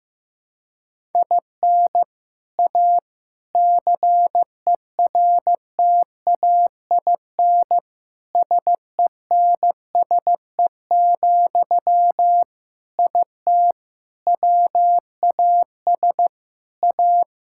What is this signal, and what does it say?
Morse code, heard as a keyed tone, about 15 wpm, 700 Hz. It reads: IN A CERTAIN SENSE, IT WAS A
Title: Sentences from Top 1000 Words - Review Part 6 - 15wpm